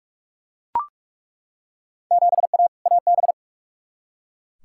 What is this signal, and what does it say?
Morse code, heard as a keyed tone, about 42 wpm, 700 Hz. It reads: 7A AB
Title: Field Day Exchanges Encore - 45wpm